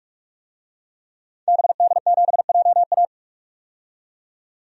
Morse code, heard as a keyed tone, about 46 wpm, 700 Hz. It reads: BD7JA
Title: DXCC-100 Call Signs - 5 Characters - 45wpm